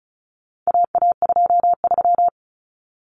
Morse code, heard as a keyed tone, about 34 wpm, 700 Hz. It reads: AA23
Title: Common Cut Numbers - 4-Digit 35wpm